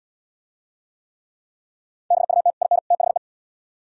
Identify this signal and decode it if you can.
Morse code, heard as a keyed tone, about 67 wpm, 700 Hz. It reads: BUT IN THE